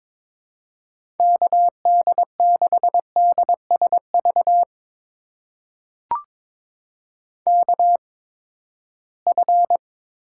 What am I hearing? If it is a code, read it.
Morse code, heard as a keyed tone, about 22 wpm, spaced as for 15 wpm, 700 Hz. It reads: KD6DSV KF
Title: USA Call Signs - 6 Characters - 8x Character Spacing - 22wpm